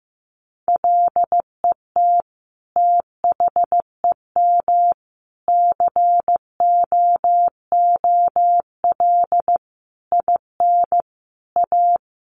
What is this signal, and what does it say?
Morse code, heard as a keyed tone, about 15 wpm, 700 Hz. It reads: LET THEM COOL IN A
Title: Sentences from Top 1200 Words 15wpm